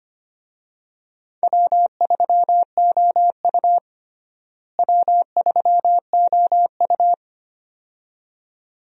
Morse code, heard as a keyed tone, about 25 wpm, 700 Hz. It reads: W3OU W3OU
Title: USA Call Signs - 4 Characters - Repeated 2x - 25wpm